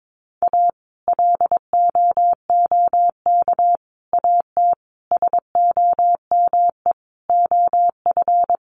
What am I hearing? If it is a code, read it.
Morse code, heard as a keyed tone, about 22 wpm, 700 Hz. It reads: A LOOK AT SOME OF
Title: Sets of 5 Words Encore2 22wpm